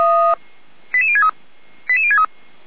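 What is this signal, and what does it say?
Five-tone selective call (ZVEI), heard as quick runs of five tones, tones of about 70 ms. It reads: 88063 88063
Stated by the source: Five-tone ZVEI selective call received on SDR.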